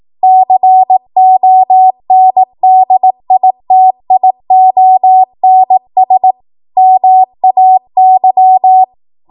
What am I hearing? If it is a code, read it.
Morse code, heard as a keyed tone, about 18 wpm, 750 Hz. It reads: CONDITIONS MAY